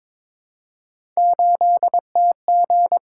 Morse code, heard as a keyed tone, about 22 wpm, 700 Hz. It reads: 8TG
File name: practice / 22wpm / Three Letter-Number Combinations - 22wpm